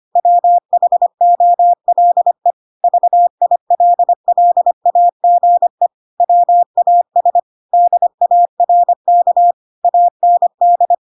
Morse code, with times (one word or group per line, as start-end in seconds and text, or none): 0.2-2.5 WHOLE
2.8-5.9 VILLAGE
6.2-7.4 WAS
7.7-9.5 DARK
9.8-10.9 AND